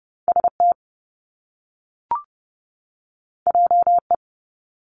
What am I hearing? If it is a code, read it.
Morse code, heard as a keyed tone, about 30 wpm, 700 Hz. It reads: S T J E
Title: Top Two Letter Combinations in Words 30wpm